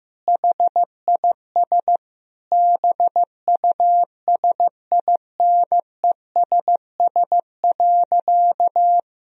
Morse code, heard as a keyed tone, about 15 wpm, 700 Hz. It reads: HIS BUSINESS.
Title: Sentences from Top 1000 Words - Review Part 3 - 15wpm